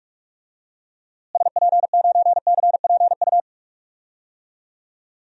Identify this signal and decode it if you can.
Morse code, heard as a keyed tone, about 45 wpm, 700 Hz. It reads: SP9CPU
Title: DXCC-100 Call Signs - 6 Characters - Repeated 3x - 45wpm